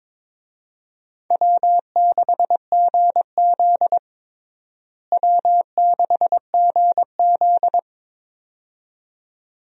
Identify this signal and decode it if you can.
Morse code, heard as a keyed tone, about 22 wpm, 700 Hz. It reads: W6GZ W6GZ
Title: USA Call Signs - 4 Characters - Repeated 2x - 22wpm